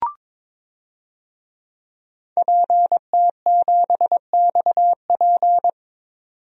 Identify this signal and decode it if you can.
Morse code, heard as a keyed tone, about 21 wpm, 700 Hz. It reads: PT7XP
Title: DXCC-100 Call Signs - 5 Characters - 22wpm